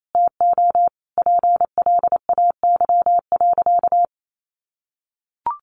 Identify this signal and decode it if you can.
Morse code, heard as a keyed tone, about 28 wpm, 700 Hz. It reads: TO PLAY.
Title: Sentences from Top 900 Words 28wpm